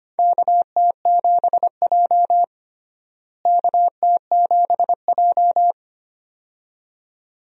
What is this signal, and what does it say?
Morse code, heard as a keyed tone, about 25 wpm, 700 Hz. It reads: KT7J KT7J
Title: USA Call Signs - 4 Characters - Repeated 2x - 25wpm